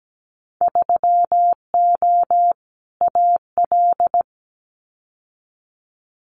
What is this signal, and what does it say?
Morse code, heard as a keyed tone, about 17 wpm, 700 Hz. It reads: 3O AL
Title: Winter Field Day Exchanges Encore 17wpm